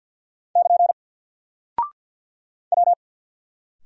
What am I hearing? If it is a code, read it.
Morse code, heard as a keyed tone, about 50 wpm, 700 Hz. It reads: <KN> W
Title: Single Character (Letter:Number:Punctuation:Prosign) 50wpm